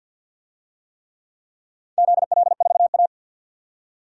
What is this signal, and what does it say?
Morse code, heard as a keyed tone, about 52 wpm, 700 Hz. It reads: ZL4A
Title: DXCC-100 Call Signs - 4 Characters - 50wpm